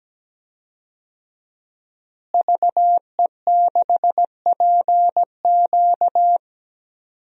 Morse code, heard as a keyed tone, about 17 wpm, 700 Hz. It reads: VE6PQ
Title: DXCC-100 Call Signs - 5 Characters - Repeated 3x - 17wpm